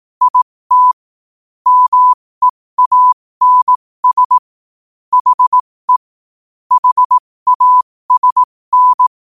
Morse code, heard as a keyed tone, about 18 wpm, spaced as for 15 wpm, 1000 Hz. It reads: IT MEANS HE HASN
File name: qotdfiles / 15wpm